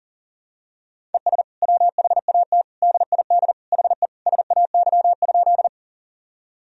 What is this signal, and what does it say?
Morse code, heard as a keyed tone, about 40 wpm, 700 Hz. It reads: ES WHAT DID HE SAY?
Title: Sentences from Top 200 Words and Abbreviations 40wpm